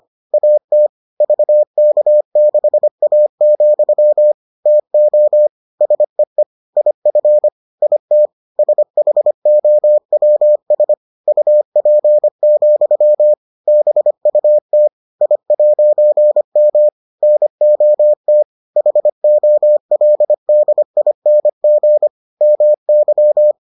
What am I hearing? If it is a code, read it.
Morse code, heard as a keyed tone, about 25 wpm, 600 Hz. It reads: AT VK6A, TO SEE IF IT SHOWS UP, BUT I'M NOT HOLDING MY